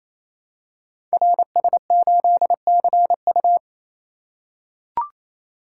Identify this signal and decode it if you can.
Morse code, heard as a keyed tone, about 29 wpm, 700 Hz. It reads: RS8CU
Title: Code Groups - 5 Letter:Numbers - 28wpm